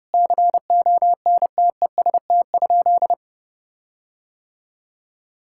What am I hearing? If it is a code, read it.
Morse code, heard as a keyed tone, about 30 wpm, 700 Hz. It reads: CONTEST?
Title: Sentences from Top 1600 Words 30wpm